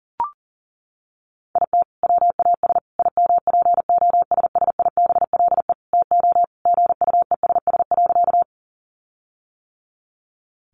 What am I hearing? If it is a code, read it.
Morse code, heard as a keyed tone, about 40 wpm, 700 Hz. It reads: IT WAS IMPOSSIBLE TO GUESS.
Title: Sentences from Top 1500 Words 40wpm